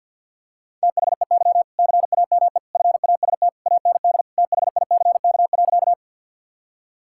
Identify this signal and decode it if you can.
Morse code, heard as a keyed tone, about 50 wpm, 700 Hz. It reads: THEY CAME FAST AND THICK.
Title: Sentences from Top 1600 Words 50wpm